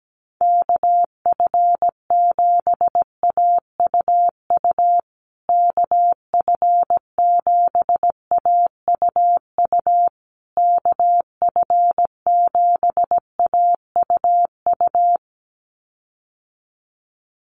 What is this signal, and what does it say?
Morse code, heard as a keyed tone, about 17 wpm, 700 Hz. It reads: KF7AUU KF7AUU KF7AUU
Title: USA Call Signs Repeated 3x 17wpm